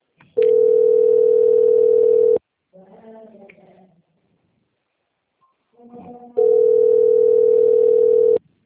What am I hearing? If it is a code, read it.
Ringback tone (North America), 440 + 480 Hz.
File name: out- - - - - .53397